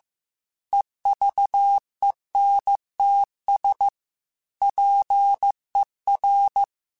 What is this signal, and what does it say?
Morse code, heard as a keyed tone, about 15 wpm, 800 Hz. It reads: EVENTS PER